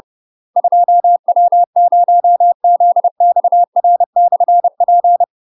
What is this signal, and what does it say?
Morse code, heard as a keyed tone, about 30 wpm, 700 Hz. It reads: 2W0ZXR/P
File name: UK MASTER QSOs TRAINING/30WPM